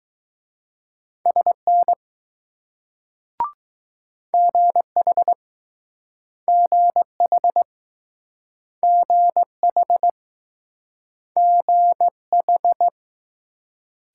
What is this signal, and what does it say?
Morse code, heard as a keyed tone, about 20 wpm, 700 Hz. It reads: SN GH GH GH GH